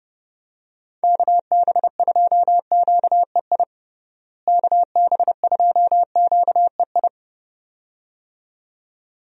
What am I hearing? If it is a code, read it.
Morse code, heard as a keyed tone, about 30 wpm, 700 Hz. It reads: KB2QEI KB2QEI
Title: USA Call Signs - 6 Characters - Repeated 2x - 30wpm